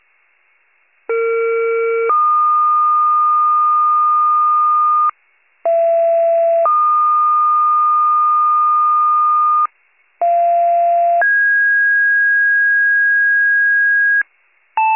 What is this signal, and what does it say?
Two-tone page, heard as a one-second tone, then a three-second tone, about 460 then 1220 Hz, about 670 then 1220 Hz, about 690 then 1650 Hz. It is heard from 1.1 to 14.2 s.